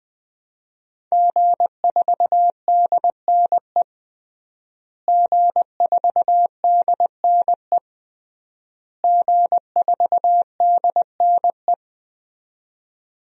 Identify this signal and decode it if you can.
Morse code, heard as a keyed tone, about 20 wpm, 700 Hz. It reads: G4DNE G4DNE G4DNE
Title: DXCC-100 Call Signs - 5 Characters - Repeated 3x - 20wpm